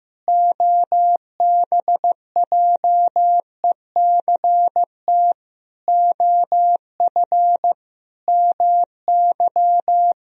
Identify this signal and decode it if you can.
Morse code, heard as a keyed tone, about 15 wpm, 700 Hz. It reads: OBJECT OF MY
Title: Sentences from Top 1000 Words - Review Part 3 - 15wpm